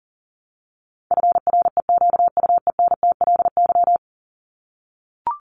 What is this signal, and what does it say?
Morse code, heard as a keyed tone, about 40 wpm, 700 Hz. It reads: FREQUENTLY
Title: LY Suffix - Top 100 Words with Base Words 40wpm